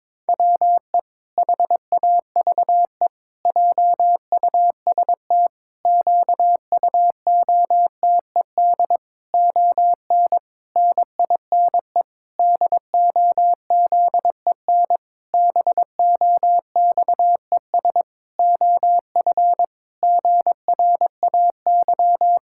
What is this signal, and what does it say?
Morse code, heard as a keyed tone, about 22 wpm, 700 Hz. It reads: WE HAVE JUST QUOTED ON NINE DOZEN BOXES OF GRAY
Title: Pangram Challenge v2 - 22wpm